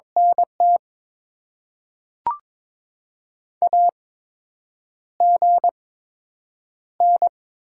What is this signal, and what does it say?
Morse code, heard as a keyed tone, about 22 wpm, 700 Hz. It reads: N T A G N
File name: Top 20 QSO Elements - 8x Character Spacing - 22wpm